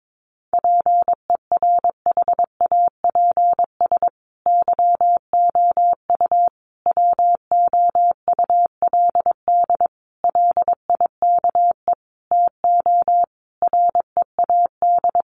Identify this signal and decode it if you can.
Morse code, heard as a keyed tone, about 22 wpm, 700 Hz. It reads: PERHAPS YOU WOULD LIKE TO READ
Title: Sentences from Top 500 Words 22wpm